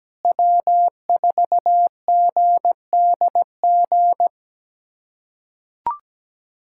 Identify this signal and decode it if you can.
Morse code, heard as a keyed tone, about 17 wpm, 700 Hz. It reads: W4GDG
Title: USA Callsign DE Callsign - 5 Characters - 17wpm